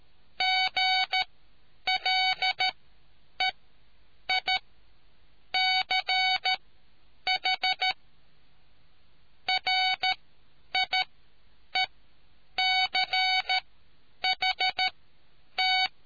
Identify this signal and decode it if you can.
Morse code, heard as a keyed tone, about 13 wpm, spaced as for 8 wpm, 750 Hz. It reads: GLEICH RIECHT